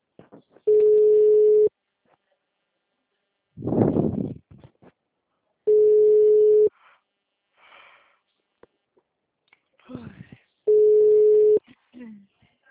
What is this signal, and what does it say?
Ringback tone (Europe, 425 Hz), with speech.
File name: out- -2005- - - .157